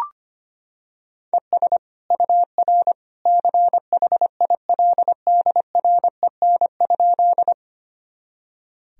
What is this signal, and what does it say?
Morse code, heard as a keyed tone, about 25 wpm, 700 Hz. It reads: ES UR CHILDREN?
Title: Sentences from Top 300 Words and Abbreviations 25wpm